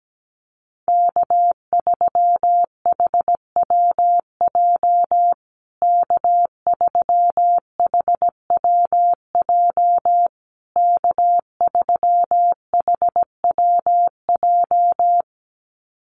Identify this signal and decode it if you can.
Morse code, heard as a keyed tone, about 17 wpm, 700 Hz. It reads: K3HWJ K3HWJ K3HWJ
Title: USA Call Signs Repeated 3x 17wpm